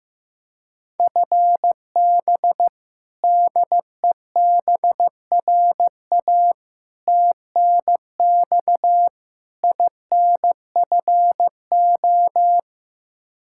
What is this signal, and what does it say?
Morse code, heard as a keyed tone, about 15 wpm, 700 Hz. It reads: FB DEBRA TNX INFO
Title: Practice QSO - Exchange 3 - More Info - 15wpm